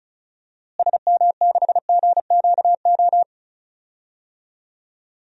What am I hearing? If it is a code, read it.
Morse code, heard as a keyed tone, about 36 wpm, 700 Hz. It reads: SM6GQO
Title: DXCC-100 Call Signs - 6 Characters - Repeated 2x - 35wpm